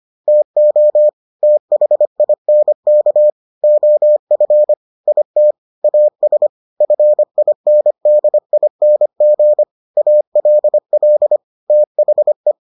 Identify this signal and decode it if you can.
Morse code, heard as a keyed tone, about 25 wpm, 600 Hz. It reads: TO THINK OF IT AS FINDING ALL THE